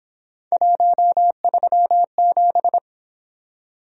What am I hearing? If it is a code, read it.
Morse code, heard as a keyed tone, about 27 wpm, 700 Hz. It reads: 137
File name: Speed-Racing - Three Letter-Number Combinations - 22wpm